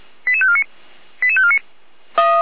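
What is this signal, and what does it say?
Five-tone selective call (ZVEI), heard as quick runs of five tones, tones of about 70 ms. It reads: 88549 88549
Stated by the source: ZVEI five-tone selective call received with an SDR.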